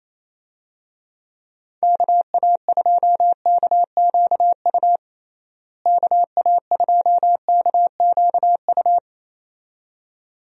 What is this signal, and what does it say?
Morse code, heard as a keyed tone, about 28 wpm, 700 Hz. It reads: KA2KQU KA2KQU
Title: USA Call Signs - 6 Characters - Repeated 2x - 28wpm